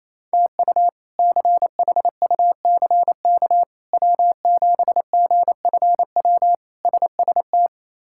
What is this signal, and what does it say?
Morse code, heard as a keyed tone, about 27 wpm, 700 Hz. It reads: TU CHUCK W7GFW SST
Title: K1USN SST QSO Practice - 28wpm